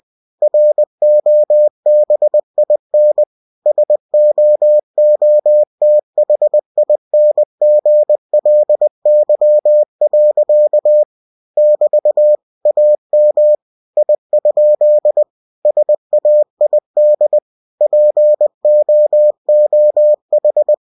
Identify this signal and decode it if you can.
Morse code, heard as a keyed tone, about 20 wpm, 600 Hz. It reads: ROBIN SOOTHINGLY. <BT> AM I? SAID POOH
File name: cw-books / winnie-the-pooh / winnie-20wpm-page-0025